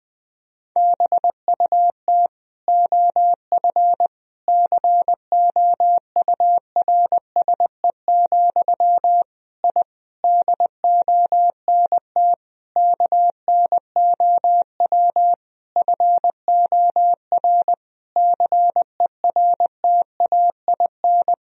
Morse code, heard as a keyed tone, about 20 wpm, 700 Hz. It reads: BUT OF COURSE, I DONT KNOW FOR CERTAIN
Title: Sentences from Top 1000 Words - Review Part 2 - 20wpm